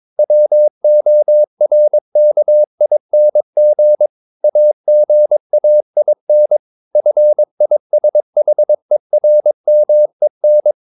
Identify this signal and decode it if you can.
Morse code, heard as a keyed tone, about 22 wpm, 600 Hz. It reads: WORKING AGAIN FISHERMEN
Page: Romanen Never af Ken Follet på engelsk.